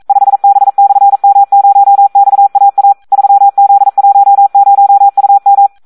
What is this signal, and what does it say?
Morse code, heard as a keyed tone, about 42 wpm, 800 Hz. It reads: 5B/M0XAA 3Z10UM
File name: Audio practica CW / SD 1 42